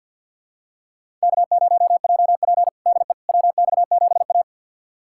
Morse code, heard as a keyed tone, about 48 wpm, 700 Hz. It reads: K0JP DE WX7A
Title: USA Callsign DE Callsign - 4 Characters - 50wpm